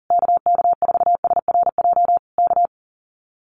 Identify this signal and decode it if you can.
Morse code, heard as a keyed tone, about 40 wpm, 700 Hz. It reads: KK4SRJ K
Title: Common QSO Phrases Encore 40wpm